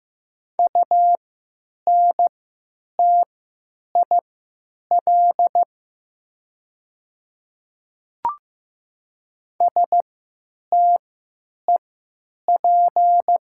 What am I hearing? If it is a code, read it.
Morse code, heard as a keyed tone, about 15 wpm, spaced as for 9 wpm, 700 Hz. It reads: UNTIL STEP
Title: 3x Character Spacing - Top 500 Words - 15wpm